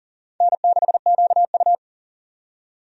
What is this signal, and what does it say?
Morse code, heard as a keyed tone, about 40 wpm, 700 Hz. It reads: N6QU
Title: USA Call Signs - 4 Characters - 40wpm